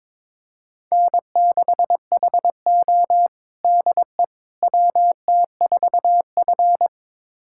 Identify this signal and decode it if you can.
Morse code, heard as a keyed tone, about 22 wpm, 700 Hz. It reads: N6HO DE WT4F
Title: USA Callsign DE Callsign - 4 Characters - 22wpm